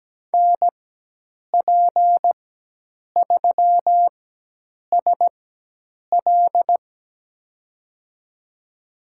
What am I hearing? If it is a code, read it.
Morse code, heard as a keyed tone, about 17 wpm, spaced as for 8 wpm, 700 Hz. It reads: NP3SL
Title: USA Call Signs - 5 Characters - 4x Character Spacing - 17wpm